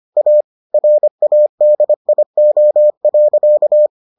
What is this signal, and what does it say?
Morse code, heard as a keyed tone, about 25 wpm, 600 Hz. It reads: A RADIO.